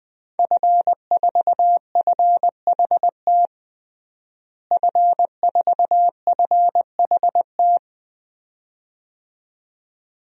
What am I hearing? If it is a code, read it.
Morse code, heard as a keyed tone, about 20 wpm, 700 Hz. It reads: F4FHT F4FHT
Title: DXCC-100 Call Signs - 5 Characters - Repeated 2x - 20wpm